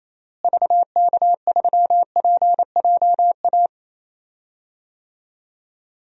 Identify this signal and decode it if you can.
Morse code, heard as a keyed tone, about 27 wpm, 700 Hz. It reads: VK3PJA
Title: DXCC-100 Call Signs - 6 Characters - Repeated 2x - 28wpm